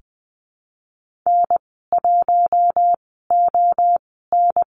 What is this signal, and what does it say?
Morse code, heard as a keyed tone, about 20 wpm, spaced as for 15 wpm, 700 Hz. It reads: N1ON
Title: 062 Lesson 11g - Letter h - Call Signs 20wpm